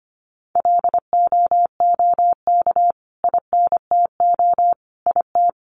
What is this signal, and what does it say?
Morse code, heard as a keyed tone, about 25 wpm, 700 Hz. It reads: LOOK INTO IT
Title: Sentences from Top 1400 Words 25wpm